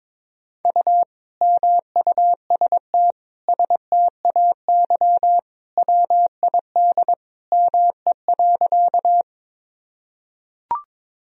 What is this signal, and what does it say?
Morse code, heard as a keyed tone, about 21 wpm, 700 Hz. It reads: U MUST STAY WID ME.
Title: Sentences from Top 500 Words and Abbreviations 22wpm